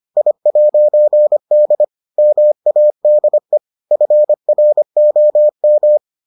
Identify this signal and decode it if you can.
Morse code, heard as a keyed tone, about 25 wpm, 600 Hz. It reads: I'D MADE FROM